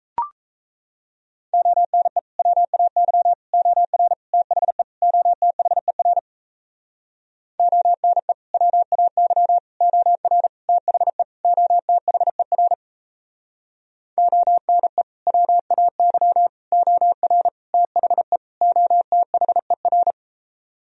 Morse code, heard as a keyed tone, about 38 wpm, 700 Hz. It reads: ONE WAY OR THE OTHER ONE WAY OR THE OTHER ONE WAY OR THE OTHER